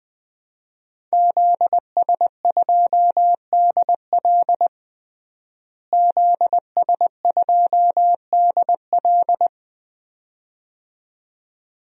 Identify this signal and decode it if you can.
Morse code, heard as a keyed tone, about 20 wpm, 700 Hz. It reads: ZS2DL ZS2DL
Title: DXCC-100 Call Signs - 5 Characters - Repeated 2x - 20wpm